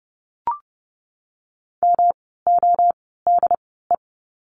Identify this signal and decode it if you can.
Morse code, heard as a keyed tone, about 30 wpm, spaced as for 17 wpm, 700 Hz. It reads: MODE
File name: course / 30:10wpm / 108 Lesson 18f - Letter m - Words 30wpm